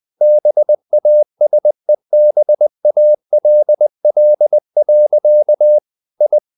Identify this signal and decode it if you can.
Morse code, heard as a keyed tone, about 20 wpm, 600 Hz. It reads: BASEBALL. I
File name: the old man and the sea / the old man and the sea part 1 20wpm